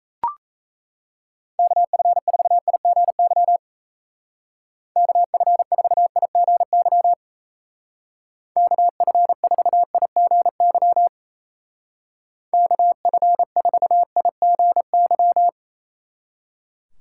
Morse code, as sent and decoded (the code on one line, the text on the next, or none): -.- ..-. ....- .. --. -.-- / -.- ..-. ....- .. --. -.-- / -.- ..-. ....- .. --. -.-- / -.- ..-. ....- .. --. -.--
KF4IGY KF4IGY KF4IGY KF4IGY